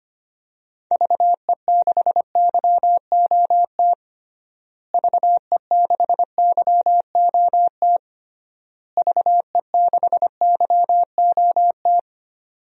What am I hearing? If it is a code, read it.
Morse code, heard as a keyed tone, about 25 wpm, 700 Hz. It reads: VE6YOT VE6YOT VE6YOT
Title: DXCC-100 Call Signs - 6 Characters - Repeated 3x - 25wpm